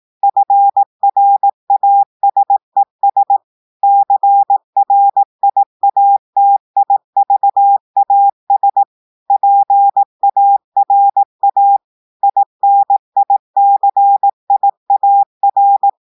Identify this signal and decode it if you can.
Morse code, heard as a keyed tone, about 18 wpm, 800 Hz. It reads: FRASES CRIATIVAS PARA INICIAR
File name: audios / portugais / portugais texte en clair 2 vitesse 18 mots minutes